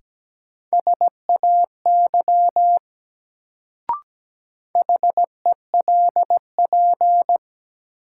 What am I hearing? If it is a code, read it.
Morse code, heard as a keyed tone, about 17 wpm, 700 Hz. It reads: SAY HELP
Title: Sentences from Top 200 Words - New Word Drill - 17wpm